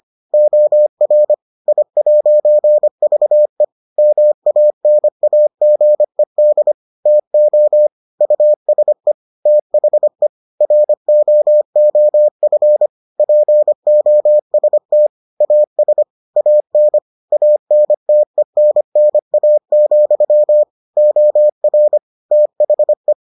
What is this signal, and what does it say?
Morse code, heard as a keyed tone, about 25 wpm, 600 Hz. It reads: OR I'VE MANAGED TO USE THE ROOF POST AS AN ANTENNA, OR THE